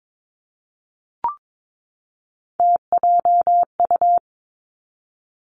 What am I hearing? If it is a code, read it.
Morse code, heard as a keyed tone, about 22 wpm, 700 Hz. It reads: TJU
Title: Three Challenging Letter-Number Combinations 22wpm